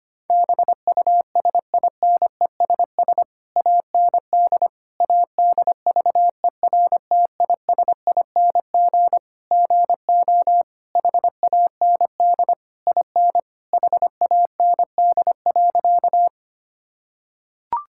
Morse code, heard as a keyed tone, about 25 wpm, 700 Hz. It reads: BUSINESS AND ADVERTISING GO HAND IN HAND.
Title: Sentences from Top 1000 Words - Review Part 6 - 25wpm